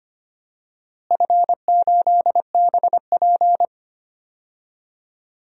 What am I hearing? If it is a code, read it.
Morse code, heard as a keyed tone, about 25 wpm, 700 Hz. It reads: F8BP
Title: DXCC-100 Call Signs - 4 Characters - Repeated 2x - 25wpm